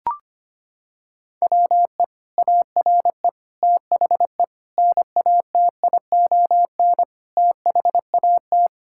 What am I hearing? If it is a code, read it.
Morse code, heard as a keyed tone, about 25 wpm, 700 Hz. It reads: WE ARE THE NATION THAT
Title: Sentences from Top 1000 Words - Review Part 6 - 25wpm